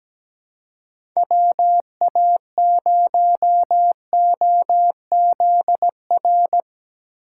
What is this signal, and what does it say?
Morse code, heard as a keyed tone, about 17 wpm, 700 Hz. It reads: WA0OZR
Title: USA Call Signs - 6 Characters - 3x Character Spacing - 17wpm